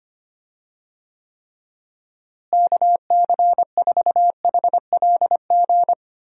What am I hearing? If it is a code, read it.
Morse code, heard as a keyed tone, about 25 wpm, 700 Hz. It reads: KC4HLG
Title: USA Call Signs - 6 Characters - Repeated 3x - 25wpm